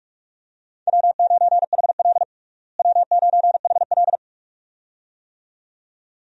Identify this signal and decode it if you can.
Morse code, heard as a keyed tone, about 42 wpm, 700 Hz. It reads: W9HL W9HL
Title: USA Call Signs - 4 Characters - Repeated 2x - 45wpm